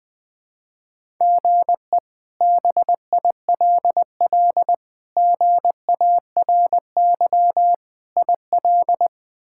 Morse code, heard as a keyed tone, about 20 wpm, 700 Hz. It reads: GE BILL GARY IL
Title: K1USN SST QSO Practice - 20wpm